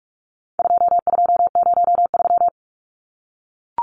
Morse code, heard as a keyed tone, about 44 wpm, 700 Hz. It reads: 2203
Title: Numbers - 4-Digit 45wpm